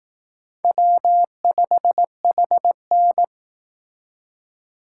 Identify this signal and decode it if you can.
Morse code, heard as a keyed tone, about 18 wpm, 700 Hz. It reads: W5HN
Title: Speed Racing - USA 1x2 Call Signs 15wpm